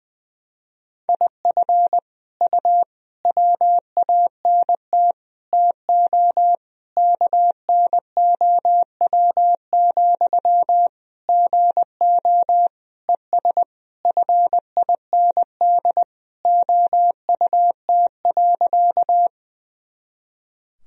Morse code, heard as a keyed tone, about 20 wpm, 700 Hz. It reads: IF U WANT TO KNOW, GO ES FIND OUT.
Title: Sentences from Top 200 Words and Abbreviations 20wpm